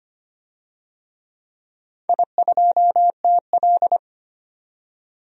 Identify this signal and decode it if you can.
Morse code, heard as a keyed tone, about 25 wpm, 700 Hz. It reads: I2TL
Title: DXCC-100 Call Signs - 4 Characters - 25wpm